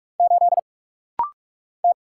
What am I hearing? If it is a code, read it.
Morse code, heard as a keyed tone, about 44 wpm, 700 Hz. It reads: 8 T
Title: Single Letter-Number - Rapid-Fire 45wpm